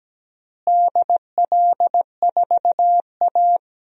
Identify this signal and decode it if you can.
Morse code, heard as a keyed tone, about 17 wpm, 700 Hz. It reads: DL4A
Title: DXCC-100 Call Signs - 4 Characters - Repeated 2x - 17wpm